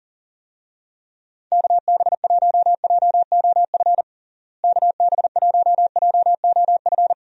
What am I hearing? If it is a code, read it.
Morse code, heard as a keyed tone, about 40 wpm, 700 Hz. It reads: KB1JOF KB1JOF
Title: USA Call Signs - 6 Characters - Repeated 2x - 40wpm